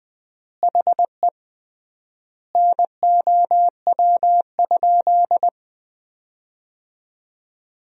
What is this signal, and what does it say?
Morse code, heard as a keyed tone, about 21 wpm, 700 Hz. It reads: HE NOW?
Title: 3x Word-Spacing - Sentences from Top 200 Words - 20wpm